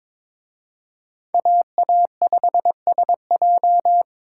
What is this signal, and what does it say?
Morse code, heard as a keyed tone, about 26 wpm, 700 Hz. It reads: AA5SJ
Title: Speed Racing - USA Call Signs 22wpm